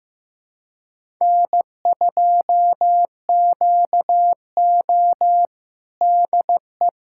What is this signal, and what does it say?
Morse code, heard as a keyed tone, about 15 wpm, 700 Hz. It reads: N2QO DE
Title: USA Callsign DE Callsign - 4 Characters - 15wpm